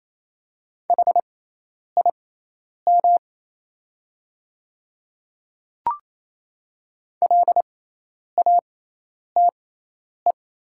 Morse code, heard as a keyed tone, about 28 wpm, spaced as for 10 wpm, 690 Hz. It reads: HIM LATE
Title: 6x Character Spacing - Top 300 Words - 28wpm